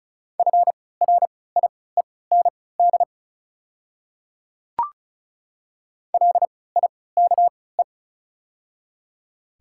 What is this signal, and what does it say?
Morse code, heard as a keyed tone, about 36 wpm, spaced as for 20 wpm, 700 Hz. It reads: FRIEND LIKE
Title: 3x Character Spacing - Top 300 Words - 35wpm